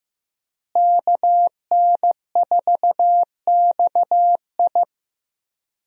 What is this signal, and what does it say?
Morse code, heard as a keyed tone, about 15 wpm, 700 Hz. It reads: KN4XI
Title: USA Call Signs - 5 Characters - Repeated 3x - 15wpm